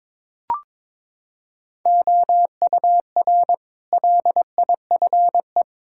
Morse code, heard as a keyed tone, about 22 wpm, 700 Hz. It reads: OUR LIFE
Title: Sentences from Top 1000 Words - Review Part 5 - 22wpm